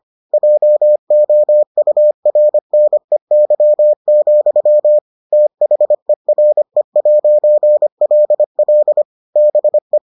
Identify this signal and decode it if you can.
Morse code, heard as a keyed tone, about 25 wpm, 600 Hz. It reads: JOURNEY, THERE'LL BE